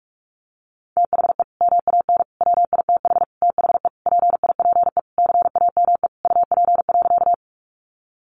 Morse code, heard as a keyed tone, about 44 wpm, 700 Hz. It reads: THE MAN WITH THE PIPE CAME UP.
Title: Sentences from Top 1500 Words 45wpm